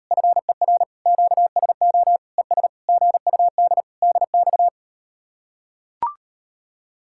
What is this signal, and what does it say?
Morse code, heard as a keyed tone, about 38 wpm, 700 Hz. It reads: FER QSO ES GUD DX